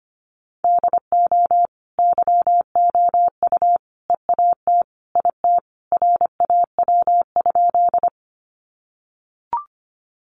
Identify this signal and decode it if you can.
Morse code, heard as a keyed tone, about 25 wpm, 700 Hz. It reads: DO YOU EAT IT RAW?
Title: Sentences from Top 1500 Words 25wpm